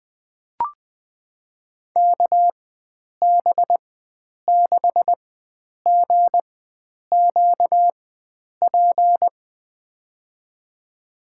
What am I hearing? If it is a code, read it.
Morse code, heard as a keyed tone, about 20 wpm, spaced as for 9 wpm, 700 Hz. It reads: KB6GQP
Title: USA Call Signs - 6 Characters - 4x Character Spacing - 20wpm